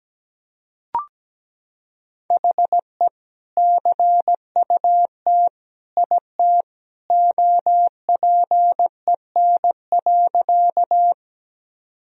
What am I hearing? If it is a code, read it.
Morse code, heard as a keyed tone, about 17 wpm, 700 Hz. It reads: HE CUT IT OPEN.
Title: Sentences from Top 1000 Words - Review Part 1 - 17wpm